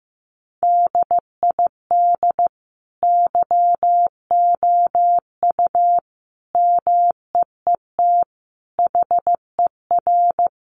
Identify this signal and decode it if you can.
Morse code, heard as a keyed tone, about 15 wpm, 700 Hz. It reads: DID YOU MEET HER
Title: Sentences from Top 1100 Words 15wpm